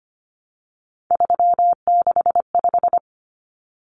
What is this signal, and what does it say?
Morse code, heard as a keyed tone, about 25 wpm, 700 Hz. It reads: 365
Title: Common Cut Numbers - 3-Digit 25wpm